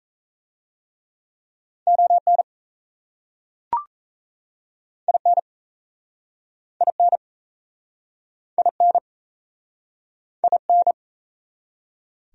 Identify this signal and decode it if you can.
Morse code, heard as a keyed tone, about 38 wpm, 700 Hz. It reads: ON IN IN IN IN